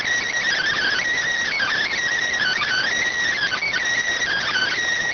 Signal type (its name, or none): Emergency Alert System (SAME) tones